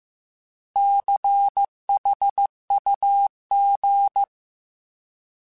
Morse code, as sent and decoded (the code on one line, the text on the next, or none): -.-. .... ..- --.
CHUG